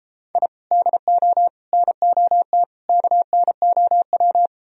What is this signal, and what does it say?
Morse code, heard as a keyed tone, about 33 wpm, 700 Hz. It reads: I DO NOT KNOW